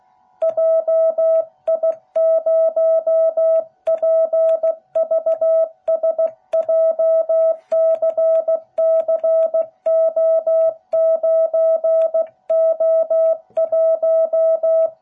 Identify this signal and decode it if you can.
Morse code, heard as a keyed tone, about 16 wpm, 660 Hz. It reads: JI0PVS JCCO9O1